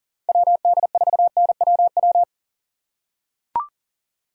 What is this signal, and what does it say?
Morse code, heard as a keyed tone, about 40 wpm, 700 Hz. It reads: WD4NWW
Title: USA Call Signs - 6 Characters - 4x Character Spacing - 40wpm